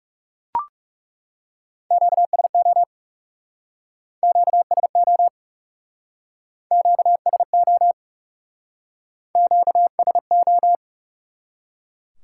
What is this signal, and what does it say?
Morse code, heard as a keyed tone, about 40 wpm, 700 Hz. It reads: QSO QSO QSO QSO